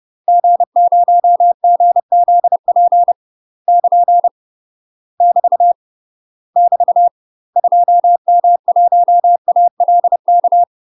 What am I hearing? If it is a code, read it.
Morse code, heard as a keyed tone, about 30 wpm, 700 Hz. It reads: G0GZP <KN> <BT> <BT> 2M1ALK
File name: UK MASTER QSOs TRAINING / 30WPM / UK MASTER QSOs 0011